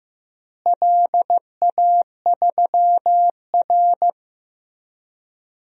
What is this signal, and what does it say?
Morse code, heard as a keyed tone, about 15 wpm, 700 Hz. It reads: LA3R
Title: DXCC-100 Call Signs - 4 Characters - Repeated 3x - 15wpm